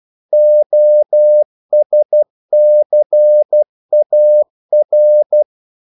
Morse code, heard as a keyed tone, about 12 wpm, 600 Hz. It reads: OSCAR
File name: CW / Practice / EnglishQuotations0312